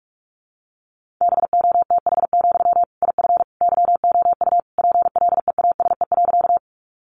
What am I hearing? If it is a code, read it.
Morse code, heard as a keyed tone, about 44 wpm, 700 Hz. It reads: BOTH, IF YOU PLEASE.
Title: Sentences from Top 1500 Words 45wpm